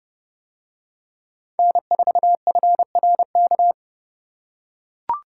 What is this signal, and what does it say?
Morse code, heard as a keyed tone, about 29 wpm, 700 Hz. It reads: N4FRK